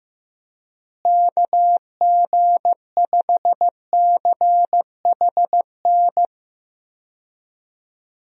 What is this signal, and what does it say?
Morse code, heard as a keyed tone, about 15 wpm, 700 Hz. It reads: KG5CHN
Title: USA Call Signs - 6 Characters - Repeated 3x - 15wpm